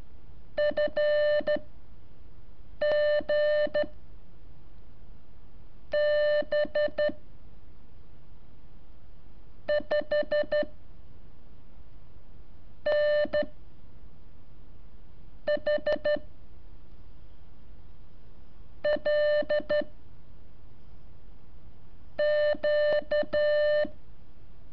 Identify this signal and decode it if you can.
Morse code, heard as a keyed tone, about 13 wpm, spaced as for 6 wpm, 600 Hz. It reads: FG B 5 N H L Q
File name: curso / ejercicio 21